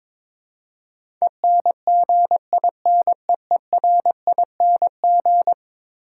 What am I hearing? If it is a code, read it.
Morse code, heard as a keyed tone, about 22 wpm, 700 Hz. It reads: ENGINEERING
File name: practice / 22wpm / Eleven Letter Words - 22wpm